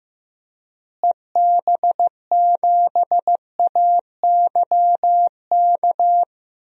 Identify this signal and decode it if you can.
Morse code, heard as a keyed tone, about 15 wpm, 700 Hz. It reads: EB7AYK
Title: DXCC-100 Call Signs - 6 Characters - Repeated 3x - 15wpm